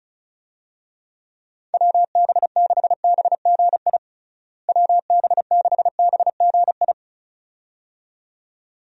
Morse code, heard as a keyed tone, about 34 wpm, 700 Hz. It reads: WB6BGI WB6BGI
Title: USA Call Signs - 6 Characters - Repeated 2x - 35wpm